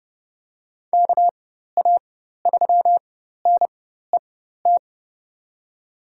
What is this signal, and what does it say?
Morse code, heard as a keyed tone, about 30 wpm, spaced as for 14 wpm, 700 Hz. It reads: KA3NET
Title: USA Call Signs - 6 Characters - 4x Character Spacing - 30wpm